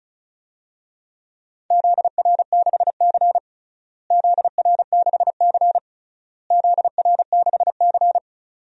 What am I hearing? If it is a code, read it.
Morse code, heard as a keyed tone, about 34 wpm, 700 Hz. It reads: ZR6C ZR6C ZR6C
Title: DXCC-100 Call Signs - 4 Characters - Repeated 3x - 35wpm